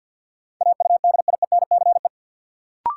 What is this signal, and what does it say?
Morse code, heard as a keyed tone, about 52 wpm, 700 Hz. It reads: AUDIENCE
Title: Sentences from Top 800 Words - New Word Drill - 50wpm